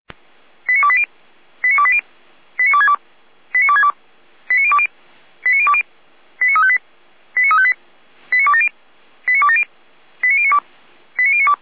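Five-tone selective call (ZVEI), heard as quick runs of five tones, tones of about 70 ms. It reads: 89280 89280 89262 89262 89020 89020 89358 89358 89270 89270 89092 89092
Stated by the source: ZVEI five-tone selective call received on SDR.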